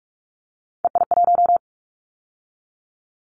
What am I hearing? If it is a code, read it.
Morse code, heard as a keyed tone, about 44 wpm, 700 Hz. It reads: EI1
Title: Callsign Prefixes for Top 50 DXCC - 45wpm